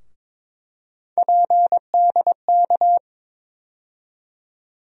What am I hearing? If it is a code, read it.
Morse code, heard as a keyed tone, about 22 wpm, 700 Hz. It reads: PDK
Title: USA Call Sign Postfixes - 3 Letters 22wpm